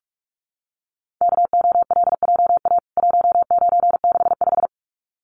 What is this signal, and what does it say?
Morse code, heard as a keyed tone, about 44 wpm, 700 Hz. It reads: KOLJA 1965
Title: CWT Exchange 45wpm